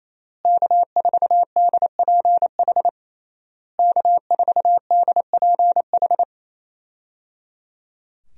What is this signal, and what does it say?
Morse code, heard as a keyed tone, about 28 wpm, 700 Hz. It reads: K4DPH K4DPH
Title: USA Call Signs - 5 Characters - Repeated 3x - 28wpm